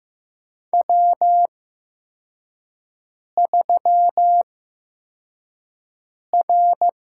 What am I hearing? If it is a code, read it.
Morse code, heard as a keyed tone, about 15 wpm, spaced as for 4 wpm, 700 Hz. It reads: W3R
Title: USA Call Signs - 4 Characters - 8x Character Spacing - 15wpm